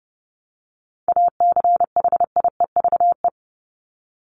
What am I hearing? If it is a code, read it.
Morse code, heard as a keyed tone, about 30 wpm, 700 Hz. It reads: ACHIEVE
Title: Top 751-1000 Words 30wpm